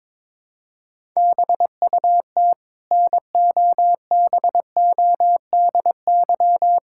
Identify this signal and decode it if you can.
Morse code, heard as a keyed tone, about 22 wpm, 700 Hz. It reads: BUT NOBODY
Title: Sentences from Top 1600 Words 22wpm